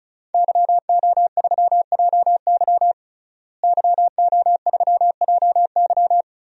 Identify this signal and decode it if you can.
Morse code, heard as a keyed tone, about 36 wpm, 700 Hz. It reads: YO3JY YO3JY
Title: DXCC-100 Call Signs - 5 Characters - Repeated 3x - 35wpm